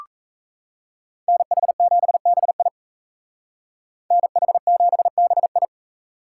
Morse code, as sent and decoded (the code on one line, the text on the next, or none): -. .... --... -... .. / -. .... --... -... ..
NH7BI NH7BI